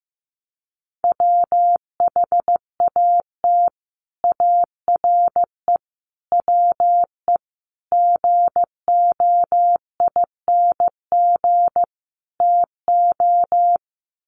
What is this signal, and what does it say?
Morse code, heard as a keyed tone, about 15 wpm, 700 Hz. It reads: WHAT ARE WE GOING TO
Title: Sets of 5 Words 15wpm